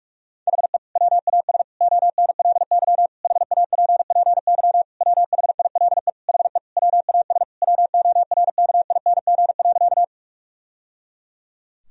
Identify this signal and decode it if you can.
Morse code, heard as a keyed tone, about 41 wpm, 700 Hz. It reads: HE WAS ONLY HAPPY WHILE HE WAS WORKING.
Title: Sentences from Top 1000 Words - Review Part 4 - 45wpm